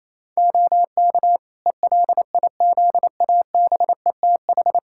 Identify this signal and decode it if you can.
Morse code, heard as a keyed tone, about 28 wpm, 700 Hz. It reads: OK ELIZABETH
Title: Practice QSO - Exchange 3 - More Info - 28wpm